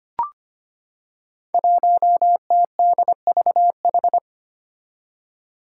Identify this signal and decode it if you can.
Morse code, heard as a keyed tone, about 25 wpm, 700 Hz. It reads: 1TDVH
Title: Five Challenging Letter-Number Combinations 25wpm